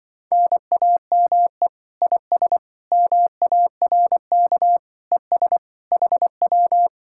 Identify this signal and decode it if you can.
Morse code, heard as a keyed tone, about 24 wpm, 700 Hz. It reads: NAME IS MARK ES HW
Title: Speed-Racing Common QSO Phrases Encore - 20wpm